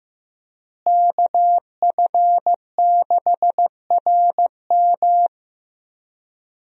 Morse code, heard as a keyed tone, about 15 wpm, 700 Hz. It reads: KF6RM
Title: USA Call Signs - 5 Characters - Repeated 3x - 15wpm